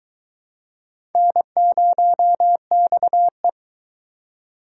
Morse code, heard as a keyed tone, about 23 wpm, 700 Hz. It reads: N0XE